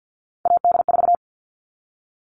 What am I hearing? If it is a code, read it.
Morse code, heard as a keyed tone, about 50 wpm, 700 Hz. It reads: AD4
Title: All Cut Numbers - 3-Digit 50wpm